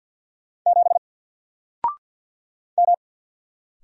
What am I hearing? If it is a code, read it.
Morse code, heard as a keyed tone, about 50 wpm, 700 Hz. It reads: 7 M
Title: Single Character (Letter:Number:Punctuation:Prosign) Alt 50wpm